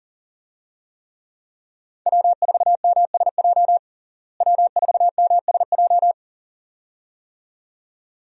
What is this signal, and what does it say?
Morse code, heard as a keyed tone, about 40 wpm, 700 Hz. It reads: W4MSJ W4MSJ
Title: USA Call Signs - 5 Characters - Repeated 2x - 40wpm